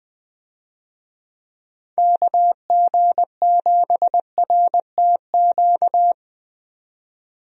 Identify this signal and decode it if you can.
Morse code, heard as a keyed tone, about 20 wpm, 700 Hz. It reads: KG7RTQ
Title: USA Call Signs - 6 Characters - 20wpm